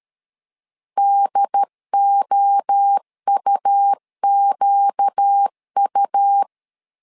Morse code, heard as a keyed tone, about 13 wpm, 800 Hz. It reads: DOUQU
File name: LEZIONI / 4 Q 12wpm 1833